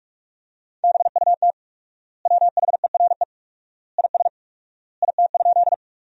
Morse code, heard as a keyed tone, about 44 wpm, 700 Hz. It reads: BUT WHERE IS IT?
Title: 4x Word-Spacing - Sentences from Top 200 Words - 45wpm